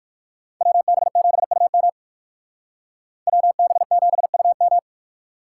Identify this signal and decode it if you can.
Morse code, heard as a keyed tone, about 53 wpm, 700 Hz. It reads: WB7UM WB7UM
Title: Speed Racing - USA 2x2 Call Signs 45wpm